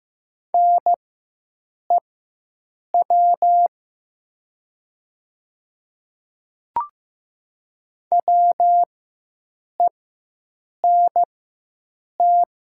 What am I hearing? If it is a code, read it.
Morse code, heard as a keyed tone, about 15 wpm, spaced as for 7 wpm, 700 Hz. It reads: NEW WENT
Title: 4x Character Spacing - Top 200 Words - 15wpm